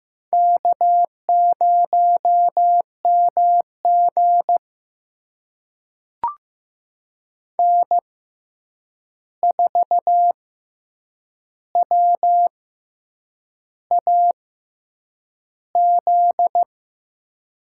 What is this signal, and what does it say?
Morse code, heard as a keyed tone, about 15 wpm, spaced as for 7 wpm, 700 Hz. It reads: K0MG N4WAZ
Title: USA Call Signs - 6x Character Spacing - 15wpm